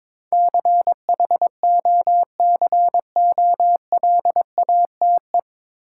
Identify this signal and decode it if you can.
Morse code, heard as a keyed tone, about 22 wpm, 700 Hz. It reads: CHOCOLATE
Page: Morse Code Ninja practice 22wpm